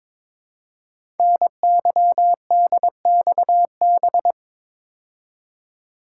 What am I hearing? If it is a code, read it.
Morse code, heard as a keyed tone, about 23 wpm, 700 Hz. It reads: NYDXB
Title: Code Groups - 5 Letters - 22wpm